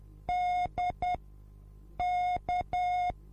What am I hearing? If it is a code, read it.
Morse code, heard as a keyed tone, about 10 wpm, spaced as for 7 wpm, 720 Hz. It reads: DK
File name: repos / Morsekurs / DJ4UF Morsen24